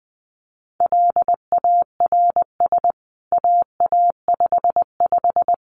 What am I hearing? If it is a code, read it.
Morse code, heard as a keyed tone, about 19 wpm, 700 Hz. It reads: LARS AA55
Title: CWT Exchange 20wpm